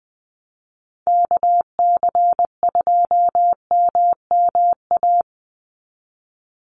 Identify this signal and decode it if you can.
Morse code, heard as a keyed tone, about 20 wpm, 700 Hz. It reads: KC2MMA
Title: USA Call Signs 20wpm